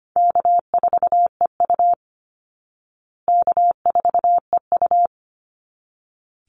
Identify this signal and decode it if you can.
Morse code, heard as a keyed tone, about 25 wpm, 700 Hz. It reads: K4EU K4EU
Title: Easy Call Signs Repeated 2x - 25wpm